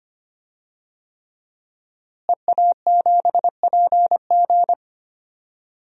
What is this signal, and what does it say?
Morse code, heard as a keyed tone, about 25 wpm, 700 Hz. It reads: EA7PG
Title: DXCC-100 Call Signs - 5 Characters - Repeated 3x - 25wpm